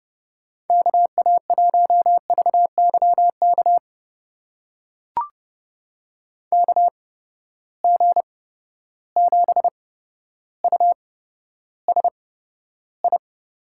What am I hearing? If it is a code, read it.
Morse code, heard as a keyed tone, about 30 wpm, spaced as for 13 wpm, 700 Hz. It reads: KA1VYK KG7USI
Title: USA Call Signs - 6 Characters - 8x Character Spacing - 30wpm